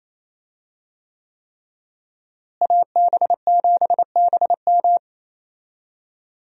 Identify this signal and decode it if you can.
Morse code, heard as a keyed tone, about 28 wpm, 700 Hz. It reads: AB7BM
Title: USA Call Signs - 5 Characters - Repeated 3x - 28wpm